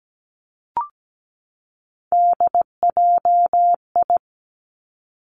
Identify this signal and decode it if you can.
Morse code, heard as a keyed tone, about 17 wpm, 700 Hz. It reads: DJI
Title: Three Challenging Letter-Number Combinations 17wpm